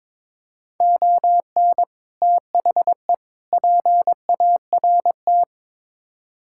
Morse code, heard as a keyed tone, about 22 wpm, 700 Hz. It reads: ON THE PART
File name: practice / 22wpm / Build A Phrase - 5-Word Phrases - 22wpm